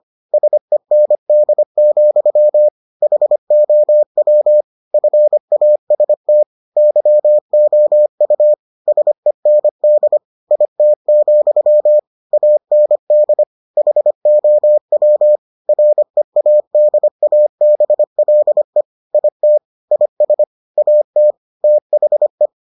Morse code, heard as a keyed tone, about 25 wpm, 600 Hz. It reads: SEND, HOW FAST YOU SEND IT, AND HOW READABLE IT IS AT THE